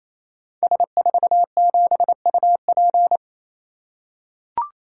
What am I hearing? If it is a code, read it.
Morse code, heard as a keyed tone, about 28 wpm, 700 Hz. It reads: S47UP
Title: Code Groups - 5 Letter:Numbers - 28wpm